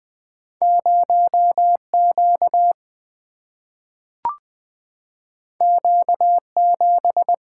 Morse code, heard as a keyed tone, about 20 wpm, 700 Hz. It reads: 0Q Q7
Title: Single-Letter Single-Number 20wpm